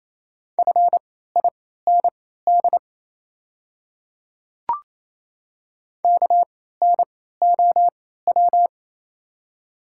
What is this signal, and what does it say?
Morse code, heard as a keyed tone, about 28 wpm, spaced as for 16 wpm, 700 Hz. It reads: FIND KNOW
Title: 3x Character Spacing - Top 200 Words - 28wpm